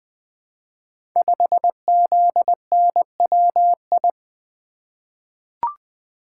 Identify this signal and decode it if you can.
Morse code, heard as a keyed tone, about 20 wpm, 700 Hz. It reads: 5ZNWI
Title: Code Groups - 5 Letter:Numbers - 20wpm